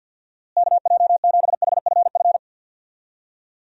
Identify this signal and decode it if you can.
Morse code, heard as a keyed tone, about 53 wpm, 700 Hz. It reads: KJ7HFF
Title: USA Call Signs - 6 Characters - Repeated 3x - 50wpm